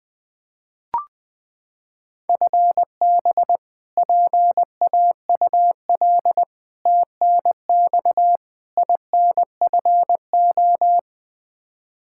Morse code, heard as a keyed tone, about 20 wpm, 700 Hz. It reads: FB PAUL TNX INFO
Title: Practice QSO - Exchange 3 - More Info - 20wpm